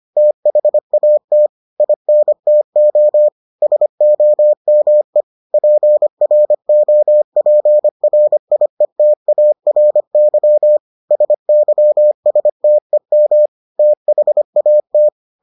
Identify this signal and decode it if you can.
Morse code, heard as a keyed tone, about 25 wpm, 600 Hz. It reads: THAT INTO SOME PROPRIETARY SYSTEM THAT